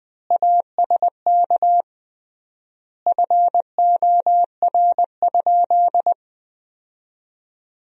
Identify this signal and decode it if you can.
Morse code, heard as a keyed tone, about 20 wpm, 700 Hz. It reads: ASK FOR?
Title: 3x Word-Spacing - Sentences from Top 200 Words - 20wpm